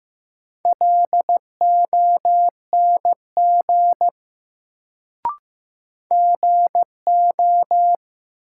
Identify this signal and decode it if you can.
Morse code, heard as a keyed tone, about 15 wpm, 700 Hz. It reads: LONG GO
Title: Sets of 2 Words - New Word Drill - 15wpm